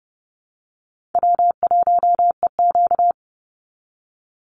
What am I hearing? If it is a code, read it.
Morse code, heard as a keyed tone, about 30 wpm, 700 Hz. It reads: W1EQ
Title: Call Signs Repeated 3x - 30wpm